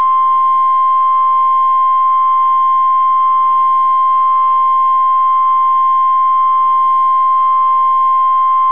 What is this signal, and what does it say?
Emergency Alert System (SAME) tones (the 1050 Hz attention tone). It is heard from 0.8 to 8.5 s.